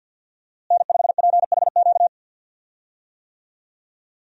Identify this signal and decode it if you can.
Morse code, heard as a keyed tone, about 46 wpm, 700 Hz. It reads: N5PHQ
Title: USA Call Signs - 5 Characters - Repeated 2x - 50wpm